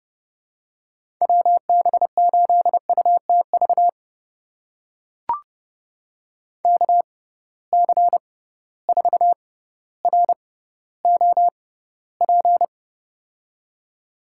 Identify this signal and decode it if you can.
Morse code, heard as a keyed tone, about 30 wpm, spaced as for 15 wpm, 700 Hz. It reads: WB8UTV KC4ROP
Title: USA Call Signs - 6 Characters - 6x Character Spacing - 30wpm